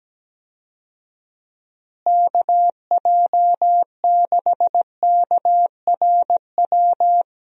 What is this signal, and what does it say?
Morse code, heard as a keyed tone, about 17 wpm, 700 Hz. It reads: KJ6KRW
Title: USA Call Signs - 6 Characters - Repeated 3x - 17wpm